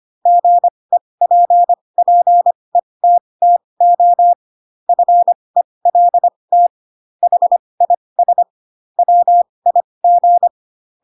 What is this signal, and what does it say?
Morse code, heard as a keyed tone, about 25 wpm, spaced as for 20 wpm, 700 Hz. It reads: GEPPETTO FELT HIS WIG
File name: Pinocchio / Pinocchio ch3 15 25 ik1hge